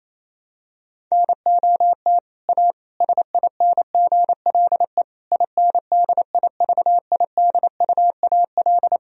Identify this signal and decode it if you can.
Morse code, heard as a keyed tone, about 27 wpm, 700 Hz. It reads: NOT A SINGLE INDIVIDUAL
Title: Sentences from Top 1000 Words - Review Part 4 - 28wpm